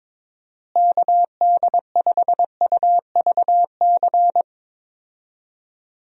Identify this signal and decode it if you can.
Morse code, heard as a keyed tone, about 22 wpm, 700 Hz. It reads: KD5UVC
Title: USA Call Signs - 6 Characters - Repeated 3x - 22wpm